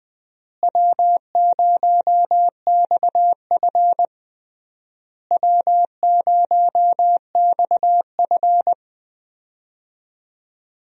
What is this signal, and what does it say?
Morse code, heard as a keyed tone, about 21 wpm, 700 Hz. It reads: W0XF W0XF
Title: USA Call Signs - 4 Characters - Repeated 3x - 20wpm